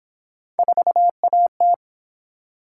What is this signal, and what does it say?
Morse code, heard as a keyed tone, about 29 wpm, 700 Hz. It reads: 4AT